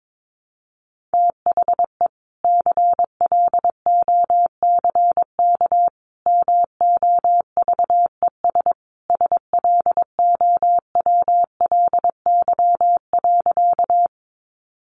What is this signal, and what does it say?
Morse code, heard as a keyed tone, about 22 wpm, 700 Hz. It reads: THE CLOCK MOVES SLOWLY.
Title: Easy Phrases 22wpm